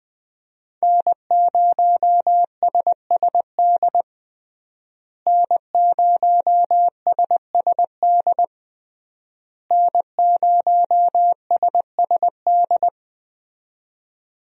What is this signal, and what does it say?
Morse code, heard as a keyed tone, about 20 wpm, 700 Hz. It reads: N0SSD N0SSD N0SSD
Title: USA Call Signs - 5 Characters - Repeated 3x - 20wpm